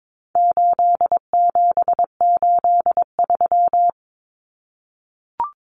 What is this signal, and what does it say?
Morse code, heard as a keyed tone, about 22 wpm, 700 Hz. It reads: 8783
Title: Common Cut Numbers - 4-Digit 22wpm